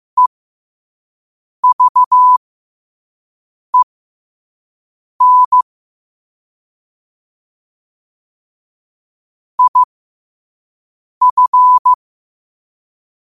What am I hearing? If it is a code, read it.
Morse code, heard as a keyed tone, about 15 wpm, spaced as for 5 wpm, 1000 Hz. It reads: EVEN IF